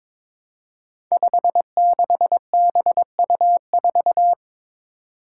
Morse code, heard as a keyed tone, about 22 wpm, 700 Hz. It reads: 56BU4
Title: Five Challenging Letter-Number Combinations 22wpm